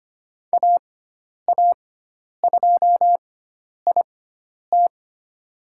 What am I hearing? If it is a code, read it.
Morse code, heard as a keyed tone, about 25 wpm, spaced as for 10 wpm, 700 Hz. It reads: AA2IT
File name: Lesson 16g - Letter c - Call Signs 25wpm